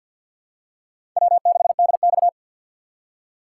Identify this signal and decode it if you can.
Morse code, heard as a keyed tone, about 53 wpm, 700 Hz. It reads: W6DX
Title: USA Call Signs - 4 Characters - 3x Character Spacing - 50wpm